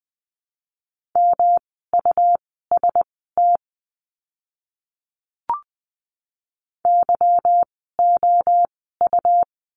Morse code, heard as a keyed tone, about 20 wpm, spaced as for 15 wpm, 700 Hz. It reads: MUST YOU
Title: 237 Lesson 38f - Letter q - Words 20wpm